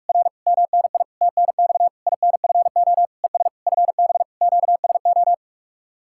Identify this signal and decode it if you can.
Morse code, heard as a keyed tone, about 57 wpm, 700 Hz. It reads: R MNI TNX INFO ES FB QSO